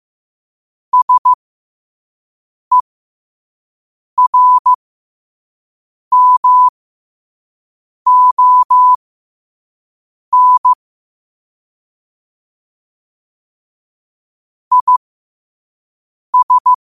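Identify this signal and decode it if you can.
Morse code, heard as a keyed tone, about 15 wpm, spaced as for 5 wpm, 1000 Hz. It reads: SERMON IS
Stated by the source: 5 wpm Morse.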